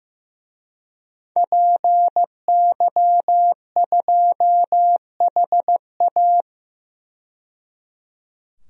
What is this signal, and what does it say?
Morse code, heard as a keyed tone, about 15 wpm, 700 Hz. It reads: PY2HA
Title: DXCC-100 Call Signs - 5 Characters - 15wpm